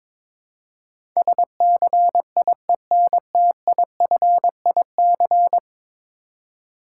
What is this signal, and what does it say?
Morse code, heard as a keyed tone, about 22 wpm, 700 Hz. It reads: SCIENTIFIC
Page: morse-code-ninja-practice-22wpm